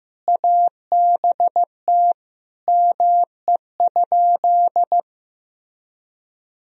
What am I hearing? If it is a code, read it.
Morse code, heard as a keyed tone, about 15 wpm, 700 Hz. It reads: ABT ME?
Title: Sentences from Top 200 Words and Abbreviations 15wpm